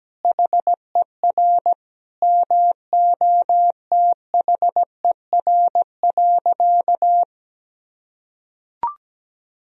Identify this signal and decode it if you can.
Morse code, heard as a keyed tone, about 17 wpm, 700 Hz. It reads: HER MOTHER.
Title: Sentences from Top 1000 Words - Review Part 4 - 17wpm